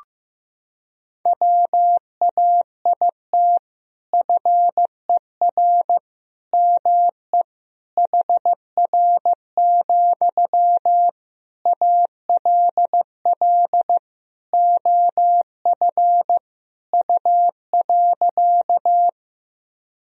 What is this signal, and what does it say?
Morse code, heard as a keyed tone, about 15 wpm, 700 Hz. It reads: WAIT FER ME HR, ALL OF U.
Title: Sentences from Top 500 Words and Abbreviations 15wpm